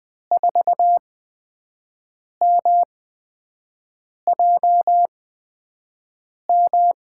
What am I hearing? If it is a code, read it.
Morse code, heard as a keyed tone, about 20 wpm, spaced as for 5 wpm, 700 Hz. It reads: 4MJM
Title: USA Call Signs - 8x Character Spacing - 20wpm